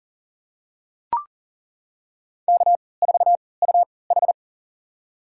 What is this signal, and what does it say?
Morse code, heard as a keyed tone, about 37 wpm, spaced as for 22 wpm, 700 Hz. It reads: K4UH
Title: USA Call Signs - 4 Characters - 3x Character Spacing - 40wpm